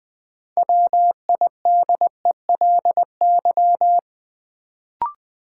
Morse code, heard as a keyed tone, about 20 wpm, 700 Hz. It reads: WIDELY
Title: Sentences from Top 1200 Words - New Word Drill - 20wpm